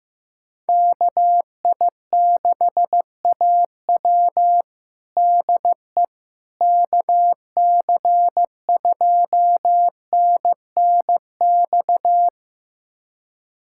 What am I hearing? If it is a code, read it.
Morse code, heard as a keyed tone, about 15 wpm, 700 Hz. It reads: KI6AW DE KC2NNX
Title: USA Callsign DE Callsign 15wpm